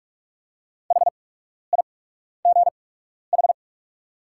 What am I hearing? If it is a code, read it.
Morse code, heard as a keyed tone, about 44 wpm, spaced as for 12 wpm, 700 Hz. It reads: HIGH
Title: 8x Character Spacing - Top 200 Words - 45wpm